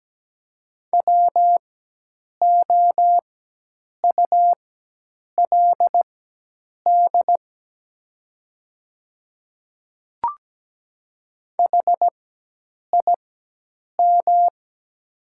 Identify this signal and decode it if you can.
Morse code, heard as a keyed tone, about 17 wpm, spaced as for 8 wpm, 700 Hz. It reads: WOULD HIM
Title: 4x Character Spacing - Top 100 Words - 17wpm